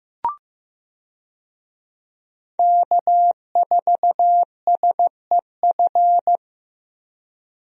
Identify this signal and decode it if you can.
Morse code, heard as a keyed tone, about 15 wpm, 700 Hz. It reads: K4SEF
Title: USA Call Signs - 5 Characters - Repeated 2x - 15wpm